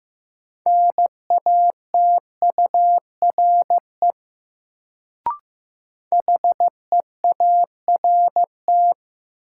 Morse code, heard as a keyed tone, about 15 wpm, 700 Hz. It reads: NATURE HEART
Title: Sets of 3 Words - New Word Drill - 15wpm